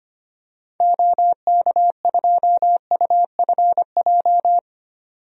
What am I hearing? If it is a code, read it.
Morse code, heard as a keyed tone, about 25 wpm, 690 Hz. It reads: OK2UFJ
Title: DXCC-100 Call Signs - 6 Characters - 25wpm